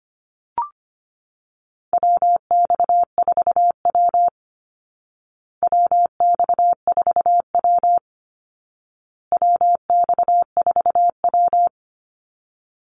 Morse code, heard as a keyed tone, about 25 wpm, 700 Hz. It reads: WX4W WX4W WX4W
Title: Call Signs Repeated 2x - 25wpm